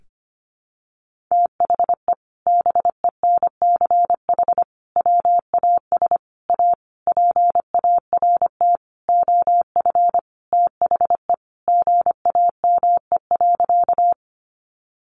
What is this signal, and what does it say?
Morse code, heard as a keyed tone, about 25 wpm, 700 Hz. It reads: THE BENCH WAS A PART OF THE GAME.
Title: Sentences from Top 1400 Words 25wpm